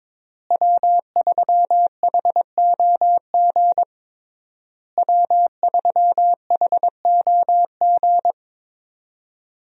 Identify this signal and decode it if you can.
Morse code, heard as a keyed tone, about 22 wpm, 690 Hz. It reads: W3HOG W3HOG
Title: USA Call Signs - 5 Characters - Repeated 3x - 22wpm